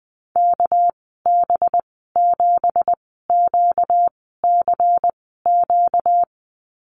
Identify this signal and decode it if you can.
Morse code, heard as a keyed tone, about 20 wpm, spaced as for 14 wpm, 700 Hz. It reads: KB7QCQ
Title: 245 Lesson 39g - Letter z - Call Signs 20wpm